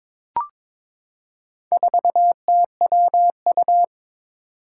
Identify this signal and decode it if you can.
Morse code, heard as a keyed tone, about 22 wpm, 700 Hz. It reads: 4TWU
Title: Four Challenging Letter-Number Combinations 22wpm